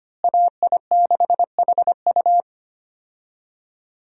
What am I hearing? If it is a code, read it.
Morse code, heard as a keyed tone, about 25 wpm, 700 Hz. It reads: AI6HU
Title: USA Call Signs - 5 Characters - Repeated 2x - 25wpm